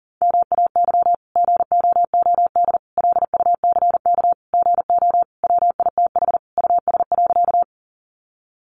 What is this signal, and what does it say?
Morse code, heard as a keyed tone, about 40 wpm, 700 Hz. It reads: MAY GOOD LUCK GO WITH US.
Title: Sentences from Top 1500 Words 40wpm